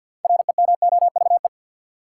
Morse code, heard as a keyed tone, about 50 wpm, 700 Hz. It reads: REMOVE
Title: Sentences from Top 800 Words - New Word Drill - 50wpm